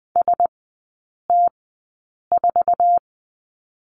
Morse code, heard as a keyed tone, about 20 wpm, spaced as for 8 wpm, 700 Hz. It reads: ST4
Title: 107 Lesson 18e - Letter m - Three Characters 20wpm